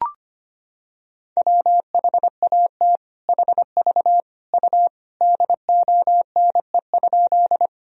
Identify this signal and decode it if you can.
Morse code, heard as a keyed tone, about 25 wpm, 700 Hz. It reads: WHAT HV U DONE?
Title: Sentences from Top 500 Words and Abbreviations 25wpm